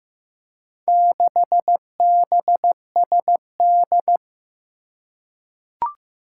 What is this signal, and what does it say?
Morse code, heard as a keyed tone, about 15 wpm, 700 Hz. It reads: 6BSD
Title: Four Challenging Letter-Number Combinations 15wpm